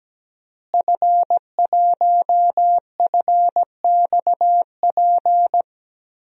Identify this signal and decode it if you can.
Morse code, heard as a keyed tone, about 17 wpm, 700 Hz. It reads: F1FXP
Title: DXCC-100 Call Signs - 5 Characters - Repeated 3x - 17wpm